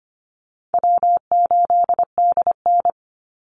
Morse code, heard as a keyed tone, about 25 wpm, 700 Hz. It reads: W8DN
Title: Call Signs - 25wpm